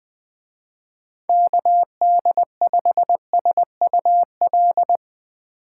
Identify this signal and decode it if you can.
Morse code, heard as a keyed tone, about 20 wpm, 700 Hz. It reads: KD5SUL